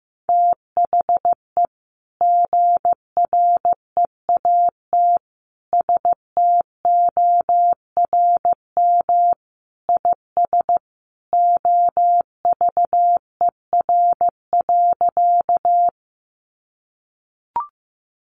Morse code, heard as a keyed tone, about 15 wpm, 700 Hz. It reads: THE GREAT STORM IS OVER.
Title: Sentences from Top 1500 Words 15wpm